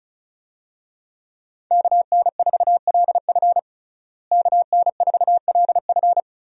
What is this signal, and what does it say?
Morse code, heard as a keyed tone, about 34 wpm, 700 Hz. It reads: KN4LF KN4LF
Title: USA Call Signs - 5 Characters - Repeated 2x - 35wpm